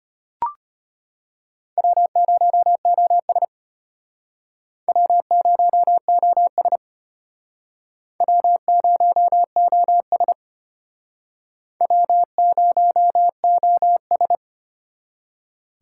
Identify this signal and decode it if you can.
Morse code, heard as a keyed tone, about 33 wpm, 700 Hz. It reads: W0OS W0OS W0OS W0OS